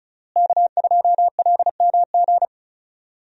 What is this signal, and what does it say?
Morse code, heard as a keyed tone, about 36 wpm, 700 Hz. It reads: K2LMG
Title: USA Call Signs - 5 Characters - Repeated 3x - 35wpm